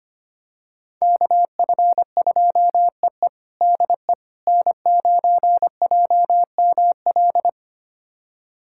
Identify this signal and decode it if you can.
Morse code, heard as a keyed tone, about 25 wpm, 700 Hz. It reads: KF2EE DE N9JML
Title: USA Callsign DE Callsign - 5 Characters - 25wpm